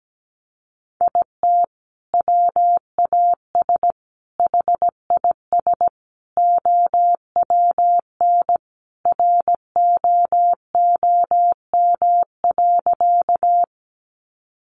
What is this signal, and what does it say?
Morse code, heard as a keyed tone, about 17 wpm, 700 Hz. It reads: IT WAS HIS OWN ROOM.
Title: Sentences from Top 300 Words 17wpm